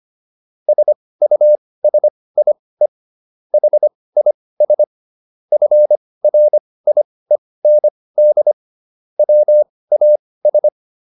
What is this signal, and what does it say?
Morse code, heard as a keyed tone, about 25 wpm, spaced as for 18 wpm, 600 Hz. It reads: SUSIE HIS FRIEND WAS